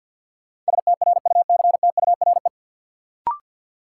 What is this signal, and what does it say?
Morse code, heard as a keyed tone, about 52 wpm, 700 Hz. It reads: STRUCTURE
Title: Sentences from Top 1000 Words - Review Part 6 - New Word Drill - 50wpm